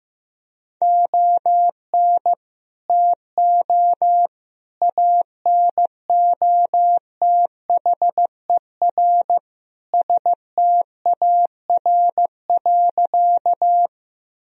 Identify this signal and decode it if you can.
Morse code, heard as a keyed tone, about 15 wpm, 700 Hz. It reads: ON TO ANOTHER STAR.
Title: Sentences from Top 1000 Words - Review Part 2 - 15wpm